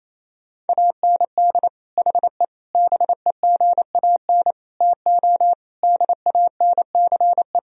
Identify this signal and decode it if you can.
Morse code, heard as a keyed tone, about 28 wpm, 700 Hz. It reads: AND HE BEGAN TO DANCE
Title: Sentences from Top 1000 Words - Review Part 4 - 28wpm